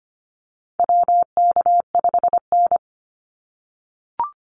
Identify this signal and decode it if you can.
Morse code, heard as a keyed tone, about 25 wpm, 700 Hz. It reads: WK5N
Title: Call Signs - 25wpm